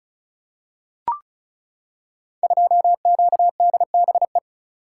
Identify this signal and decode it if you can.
Morse code, heard as a keyed tone, about 36 wpm, 700 Hz. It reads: 2QDBE
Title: Code Groups - 5 Letter:Numbers - 35wpm